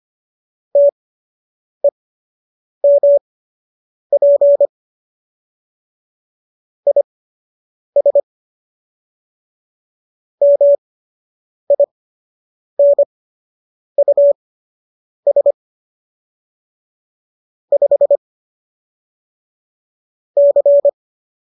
Morse code, heard as a keyed tone, about 25 wpm, spaced as for 8 wpm, 570 Hz. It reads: TEMP IS MINUS 5 C